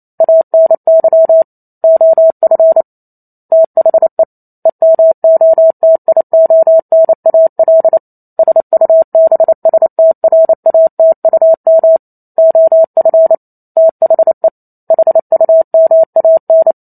Morse code, heard as a keyed tone, about 28 wpm, 650 Hz. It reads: ANY OF THE EMOTIONAL SUBSTRATUM OF THE HUMAN